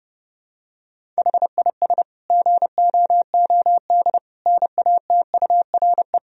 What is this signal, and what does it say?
Morse code, heard as a keyed tone, about 30 wpm, 700 Hz. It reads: HIS GOOD NATURE
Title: Sentences from Top 1000 Words - Review Part 1 - 30wpm